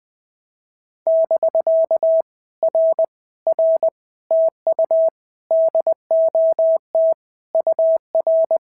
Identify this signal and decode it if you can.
Morse code, heard as a keyed tone, about 19 wpm, 650 Hz. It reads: <BK> R R TU DOT UR
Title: POTA QSO Practice - 20wpm